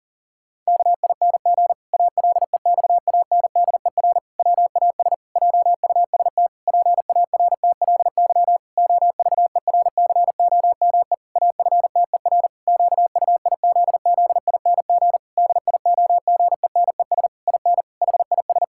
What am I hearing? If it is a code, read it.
Morse code, heard as a keyed tone, about 40 wpm, 700 Hz. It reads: KING ALEXANDER WAS JUST PARTLY OVERCOME AFTER QUIZZING DIOGENES IN HIS
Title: Pangram Challenge v2 - 40wpm